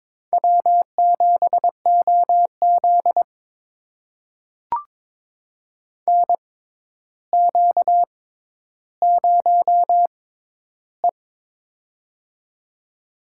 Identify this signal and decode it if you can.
Morse code, heard as a keyed tone, about 22 wpm, spaced as for 11 wpm, 700 Hz. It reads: W7OZ NQ0E
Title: USA Call Signs - 4 Characters - 6x Character Spacing - 22wpm